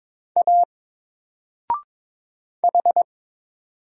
Morse code, heard as a keyed tone, about 22 wpm, 700 Hz. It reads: A H
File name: Single-Letter-Number-22wpm